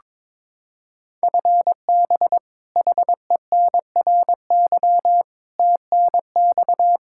Morse code, heard as a keyed tone, about 23 wpm, 700 Hz. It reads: FB HENRY TNX
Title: Practice QSO - Exchange 3 - More Info - 22wpm